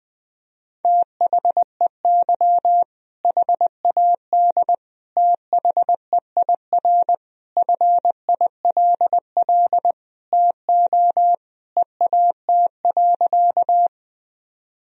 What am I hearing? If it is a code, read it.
Morse code, heard as a keyed tone, about 20 wpm, 700 Hz. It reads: THEY HAD THEIR FILL TO EAT.
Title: Sentences from Top 1000 Words - Review Part 3 - 20wpm